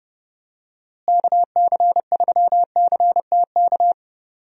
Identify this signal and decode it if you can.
Morse code, heard as a keyed tone, about 30 wpm, 700 Hz. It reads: KC3CTK
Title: USA Call Signs - 6 Characters - Repeated 3x - 30wpm